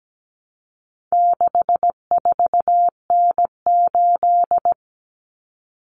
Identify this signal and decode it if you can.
Morse code, heard as a keyed tone, about 17 wpm, 700 Hz. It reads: 64N8
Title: Common Cut Numbers - 4-Digit 17wpm